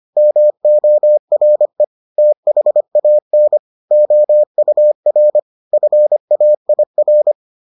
Morse code, heard as a keyed tone, about 25 wpm, 600 Hz. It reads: MORE THAN OUR FAIR